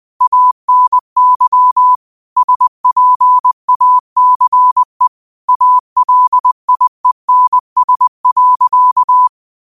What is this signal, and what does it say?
Morse code, heard as a keyed tone, about 20 wpm, 1000 Hz. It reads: ANY SPACE ALIENS.
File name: qotdfiles / 20wpm